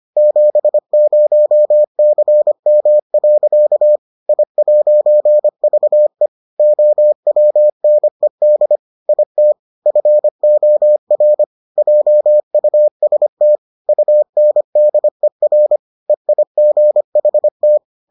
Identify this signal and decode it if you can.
Morse code, heard as a keyed tone, about 25 wpm, 600 Hz. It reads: 70CM. I'VE OWNED IT FOR JUST UNDER EIGHT